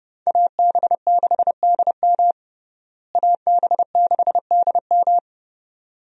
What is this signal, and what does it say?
Morse code, heard as a keyed tone, about 30 wpm, 700 Hz. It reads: AB6DM AB6DM
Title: USA Call Signs - 5 Characters - Repeated 2x - 30wpm